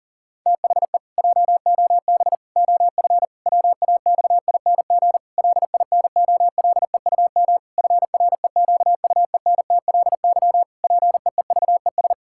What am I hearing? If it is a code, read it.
Morse code, heard as a keyed tone, about 40 wpm, 700 Hz. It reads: THE JOB OF WAXING LINOLEUM FREQUENTLY PEEVES
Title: Pangram Challenge v2 - 40wpm